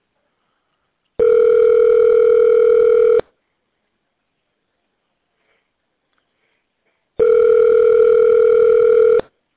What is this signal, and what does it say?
Ringback tone (North America), 440 + 480 Hz.